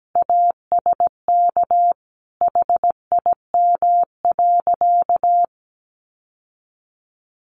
Morse code, heard as a keyed tone, about 17 wpm, 700 Hz. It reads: ASK HIM.
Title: Sentences from Top 1000 Words 17wpm